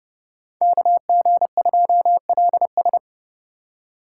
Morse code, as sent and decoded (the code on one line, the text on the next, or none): -.- --. ..--- .-.. ...
KG2LS